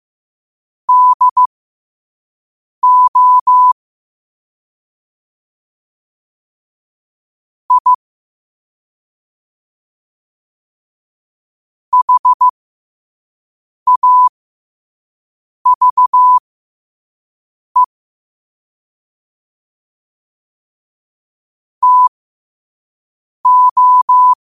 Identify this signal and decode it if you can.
Morse code, heard as a keyed tone, about 15 wpm, spaced as for 5 wpm, 1000 Hz. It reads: DO I HAVE TO
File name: qotdfiles / 5wpm